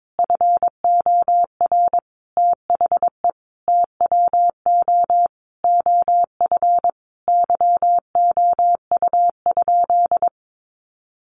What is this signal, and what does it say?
Morse code, heard as a keyed tone, about 22 wpm, 700 Hz. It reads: FOR THE TWO OF YOU?
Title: Sentences from Top 1400 Words 22wpm